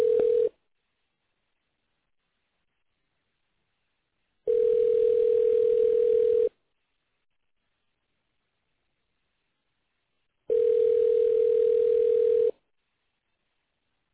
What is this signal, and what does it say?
Ringback tone (North America), 440 + 480 Hz.